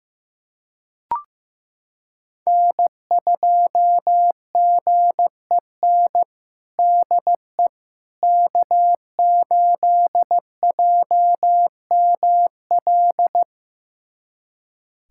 Morse code, heard as a keyed tone, about 15 wpm, 700 Hz. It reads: N2GEN DE K8JML
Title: USA Callsign DE Callsign - 5 Characters - 15wpm